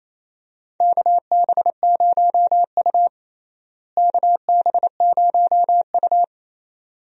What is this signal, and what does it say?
Morse code, heard as a keyed tone, about 28 wpm, 700 Hz. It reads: KB0U KB0U
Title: USA Call Signs - 4 Characters - Repeated 2x - 28wpm